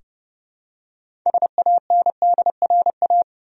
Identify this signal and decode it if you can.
Morse code, heard as a keyed tone, about 30 wpm, 700 Hz. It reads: SANDRA
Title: USA Common Names - 30wpm